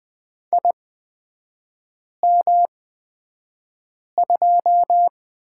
Morse code, heard as a keyed tone, about 20 wpm, spaced as for 5 wpm, 700 Hz. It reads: IM2
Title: 255 Lesson 41e - Period - Three Characters 20wpm